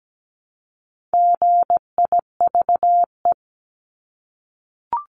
Morse code, heard as a keyed tone, about 17 wpm, 700 Hz. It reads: GIVE
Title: Multi-letter Phonemes Repeated 3x 17wpm